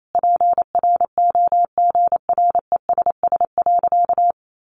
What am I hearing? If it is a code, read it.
Morse code, heard as a keyed tone, about 28 wpm, 700 Hz. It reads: PROGRESS.
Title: Sentences from Top 1400 Words 28wpm